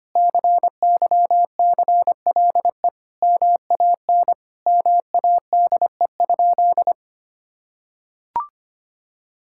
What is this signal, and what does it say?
Morse code, heard as a keyed tone, about 25 wpm, 700 Hz. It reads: CYCLE MAN MADE?
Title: Sentences from Top 1000 Words - Review Part 6 - 25wpm